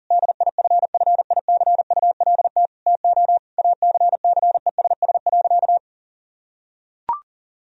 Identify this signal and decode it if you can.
Morse code, heard as a keyed tone, about 40 wpm, 700 Hz. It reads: DIFFICULT TO ACCESS.
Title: Sentences from Top 1000 Words - Review Part 4 - 40wpm